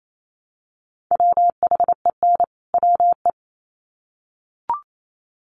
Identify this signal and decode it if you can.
Morse code, heard as a keyed tone, about 28 wpm, 700 Hz. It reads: WHEN WE
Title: Sets of 2 Words 28wpm